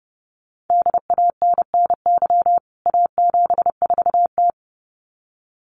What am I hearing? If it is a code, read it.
Morse code, heard as a keyed tone, about 28 wpm, 700 Hz. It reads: DANNY A74T
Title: CWT Exchange 30wpm